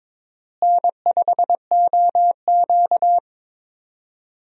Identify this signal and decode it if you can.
Morse code, heard as a keyed tone, about 21 wpm, 700 Hz. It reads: N5OQ
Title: USA Call Signs - 4 Characters - Repeated 2x - 22wpm